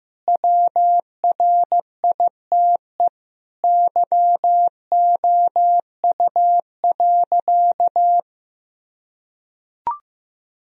Morse code, heard as a keyed tone, about 15 wpm, 700 Hz. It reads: WRITE YOU.
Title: Sentences from Top 1000 Words - Review Part 1 - 15wpm